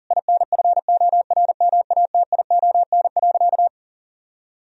Morse code, heard as a keyed tone, about 40 wpm, 700 Hz. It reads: INFORMATION.
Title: Sentences from Top 1000 Words - Review Part 3 - 40wpm